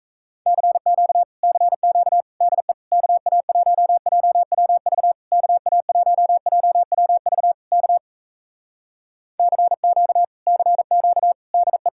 Morse code, heard as a keyed tone, about 38 wpm, 700 Hz. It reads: CQ CQ DE KA1JWV KA1JWV K CQ CQ DE